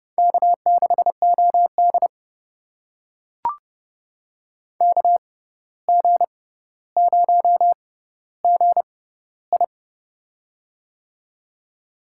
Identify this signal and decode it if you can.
Morse code, heard as a keyed tone, about 32 wpm, spaced as for 15 wpm, 700 Hz. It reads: K6OD KG0GI
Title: USA Call Signs - 6x Character Spacing - 30wpm